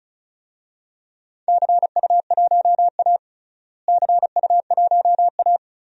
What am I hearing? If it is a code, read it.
Morse code, heard as a keyed tone, about 34 wpm, 700 Hz. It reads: CU1A CU1A
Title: DXCC-100 Call Signs - 4 Characters - Repeated 2x - 35wpm